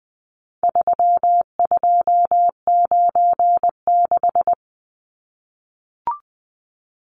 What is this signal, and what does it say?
Morse code, heard as a keyed tone, about 20 wpm, 700 Hz. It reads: 3296
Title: Numbers - 4-Digit 20wpm